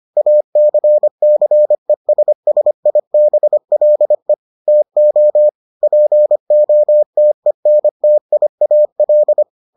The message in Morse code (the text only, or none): ACCESSIBLE TO POTENTIAL